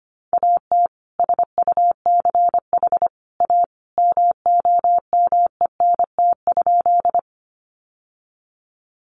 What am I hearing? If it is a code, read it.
Morse code, heard as a keyed tone, about 25 wpm, 700 Hz. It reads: AT SUCH A MOMENT?
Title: Sentences from Top 1000 Words 25wpm